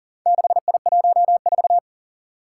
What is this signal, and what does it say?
Morse code, heard as a keyed tone, about 40 wpm, 700 Hz. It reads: 6I14
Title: Four Challenging Letter-Number Combinations 40wpm